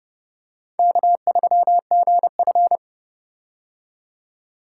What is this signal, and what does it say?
Morse code, heard as a keyed tone, about 30 wpm, 700 Hz. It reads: K3GF